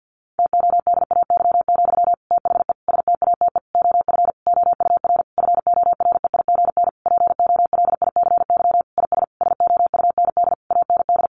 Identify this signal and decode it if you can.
Morse code, heard as a keyed tone, about 50 wpm, 700 Hz. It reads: TODAY, THE STATE OF OUR FOREIGN POLICY IS SOUND AND
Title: Sentences from Top 1400 Words 50wpm